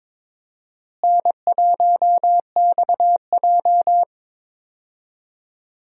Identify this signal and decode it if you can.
Morse code, heard as a keyed tone, about 22 wpm, 700 Hz. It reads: N1XJ
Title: USA Call Signs - 4 Characters - Repeated 2x - 22wpm